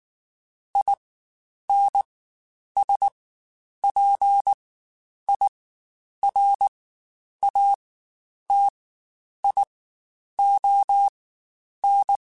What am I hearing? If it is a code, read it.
Morse code, heard as a keyed tone, about 19 wpm, spaced as for 9 wpm, 800 Hz. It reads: INSPIRATION
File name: Quotes Spaced / Quotes Spaced-mp3s / Persperation sp @18